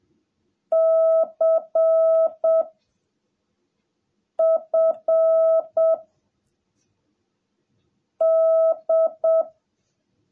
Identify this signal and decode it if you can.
Morse code, heard as a keyed tone, about 7 wpm, 660 Hz. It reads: C F D
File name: jh8cbh / 2021cw / cw17a